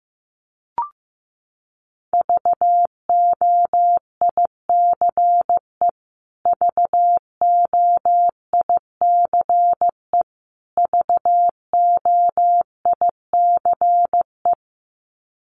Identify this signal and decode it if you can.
Morse code, heard as a keyed tone, about 15 wpm, 700 Hz. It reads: VOICE VOICE VOICE
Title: Top 500 Words Repeated 15wpm